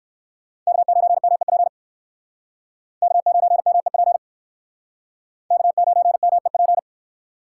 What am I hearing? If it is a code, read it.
Morse code, heard as a keyed tone, about 62 wpm, 700 Hz. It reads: K9MEP K9MEP K9MEP